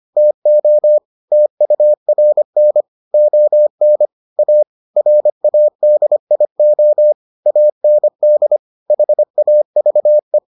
Morse code, heard as a keyed tone, about 25 wpm, 600 Hz. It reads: TO TURN ON A RADIO AND HAVE